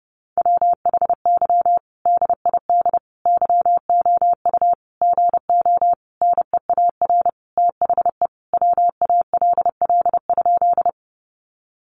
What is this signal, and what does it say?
Morse code, heard as a keyed tone, about 30 wpm, 700 Hz. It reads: WHY DID YOU GO NEAR THE WALL?
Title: Sentences from Top 1100 Words 30wpm